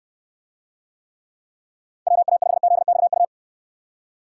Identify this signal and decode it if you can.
Morse code, heard as a keyed tone, about 69 wpm, 700 Hz. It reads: WA5ZXU